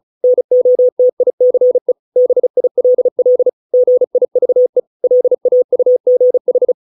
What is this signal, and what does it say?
Morse code, heard as a keyed tone, about 35 wpm, 500 Hz. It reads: NOTICE BILL GIVE LAUGH